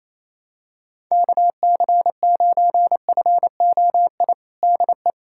Morse code, heard as a keyed tone, about 28 wpm, 700 Hz. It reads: KC9FOI DE
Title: USA Callsign DE Callsign - 6 Characters - 28wpm